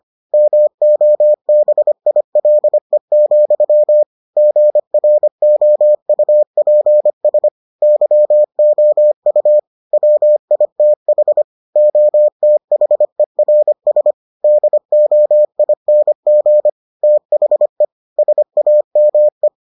Morse code, heard as a keyed tone, about 25 wpm, 600 Hz. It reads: MOBILE, GROUPS YOU WITH OTHERS DOING THE SAME